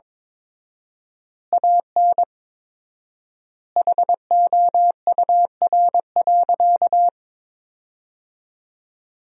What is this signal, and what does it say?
Morse code, heard as a keyed tone, about 21 wpm, 690 Hz. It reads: AN HOUR.
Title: 4x Word-Spacing - Sentences from Top 400 Words - 22wpm